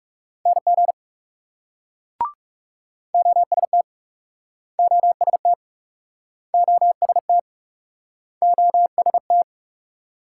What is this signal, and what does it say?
Morse code, heard as a keyed tone, about 38 wpm, 700 Hz. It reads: NG OST OST OST OST